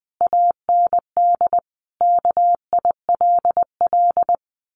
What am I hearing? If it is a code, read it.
Morse code, heard as a keyed tone, about 20 wpm, 700 Hz. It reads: AND KILL
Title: A Sound Of Thunder 20wpm